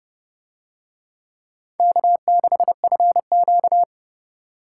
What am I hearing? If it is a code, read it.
Morse code, heard as a keyed tone, about 28 wpm, 700 Hz. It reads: K6FQ